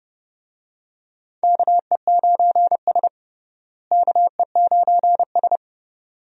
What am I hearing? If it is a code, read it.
Morse code, heard as a keyed tone, about 31 wpm, 700 Hz. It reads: KE9S KE9S
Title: USA Call Signs - 4 Characters - Repeated 2x - 30wpm